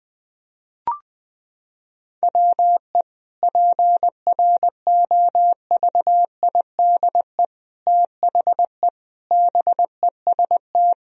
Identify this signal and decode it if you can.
Morse code, heard as a keyed tone, about 20 wpm, 700 Hz. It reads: WE PROVIDE THE BEST
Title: Sentences from Top 1000 Words - Review Part 4 - 20wpm